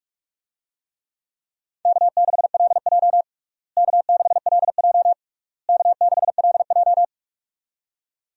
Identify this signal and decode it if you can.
Morse code, heard as a keyed tone, about 46 wpm, 700 Hz. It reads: K6LJ K6LJ K6LJ
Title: USA Call Signs - 4 Characters - Repeated 3x - 45wpm